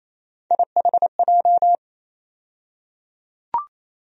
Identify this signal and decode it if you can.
Morse code, heard as a keyed tone, about 28 wpm, 700 Hz. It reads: IHJ
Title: USA Call Sign Postfixes - 3 Letters 28wpm